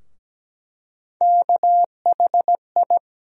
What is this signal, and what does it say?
Morse code, heard as a keyed tone, about 17 wpm, 700 Hz. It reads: KHI
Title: USA Call Sign Postfixes - 3 Letters 17wpm